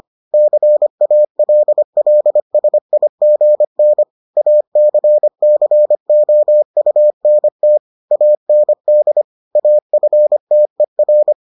Morse code, heard as a keyed tone, about 25 wpm, 600 Hz. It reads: CALLSIGN ACCOUNT AND AFTER